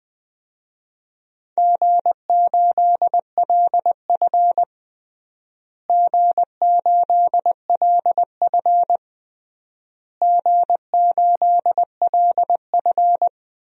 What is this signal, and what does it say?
Morse code, heard as a keyed tone, about 20 wpm, 700 Hz. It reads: G8LF G8LF G8LF
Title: DXCC-100 Call Signs - 4 Characters - Repeated 3x - 20wpm